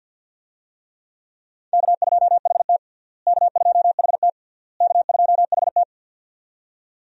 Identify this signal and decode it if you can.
Morse code, heard as a keyed tone, about 52 wpm, 700 Hz. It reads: K2HT K2HT K2HT
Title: USA Call Signs - 4 Characters - Repeated 3x - 50wpm